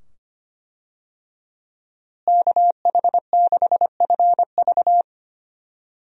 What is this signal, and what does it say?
Morse code, heard as a keyed tone, about 25 wpm, 700 Hz. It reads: KH6FV
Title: USA Call Signs - 5 Characters - Repeated 2x - 25wpm